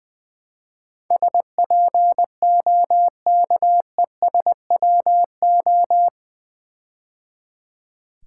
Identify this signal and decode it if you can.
Morse code, heard as a keyed tone, about 20 wpm, 700 Hz. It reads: SPOKESWO
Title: Build A Word - 4 Letters At A Time - 11-Letter Words - 20wpm